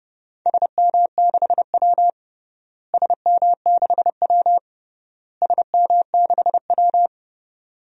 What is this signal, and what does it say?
Morse code, heard as a keyed tone, about 32 wpm, 700 Hz. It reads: SM6W SM6W SM6W
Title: DXCC-100 Call Signs - 4 Characters - Repeated 3x - 30wpm